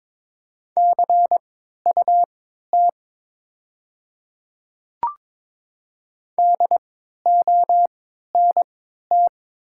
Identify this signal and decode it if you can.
Morse code, heard as a keyed tone, about 22 wpm, spaced as for 13 wpm, 700 Hz. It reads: CUT DONT
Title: 3x Character Spacing - Top 300 Words - 22wpm